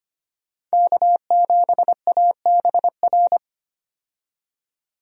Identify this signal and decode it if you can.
Morse code, heard as a keyed tone, about 25 wpm, 700 Hz. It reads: K7ABR
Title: USA Call Signs - 5 Characters - 25wpm